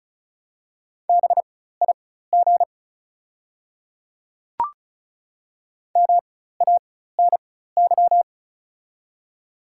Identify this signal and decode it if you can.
Morse code, heard as a keyed tone, about 35 wpm, spaced as for 16 wpm, 690 Hz. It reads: BIG MANY
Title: 4x Character Spacing - Top 200 Words - 35wpm